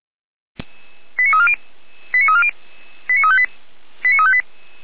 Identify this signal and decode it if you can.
Five-tone selective call (ZVEI), heard as quick runs of five tones, tones of about 80 ms. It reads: 89340 89340 89358